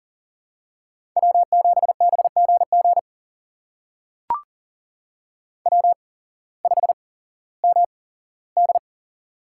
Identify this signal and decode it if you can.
Morse code, heard as a keyed tone, about 42 wpm, spaced as for 19 wpm, 700 Hz. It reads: W7BGG W5MD
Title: USA Call Signs - 8x Character Spacing - 40wpm